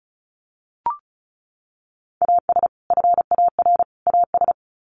Morse code, heard as a keyed tone, about 34 wpm, 700 Hz. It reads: AS FAR AS
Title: Sets of 3 Words 35wpm